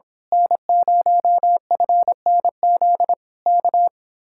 Morse code, heard as a keyed tone, about 25 wpm, 700 Hz. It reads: N0FNZ K
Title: Speed-Racing Common QSO Phrases Encore - 22wpm